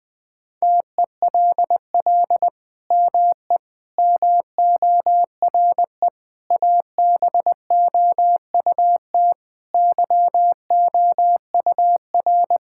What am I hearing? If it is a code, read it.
Morse code, heard as a keyed tone, about 20 wpm, 700 Hz. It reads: TELL ME MORE ABOUT YOUR
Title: Sentences from Top 1000 Words - Review Part 5 - 20wpm